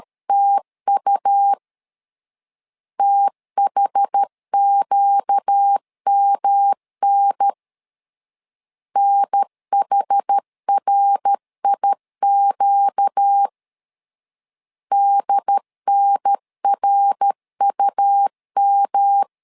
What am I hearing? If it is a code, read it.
Morse code, heard as a keyed tone, about 13 wpm, 800 Hz. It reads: TU THQMN NHRIQ DNRUM
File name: LEZIONI / 4 Q 12wpm 1833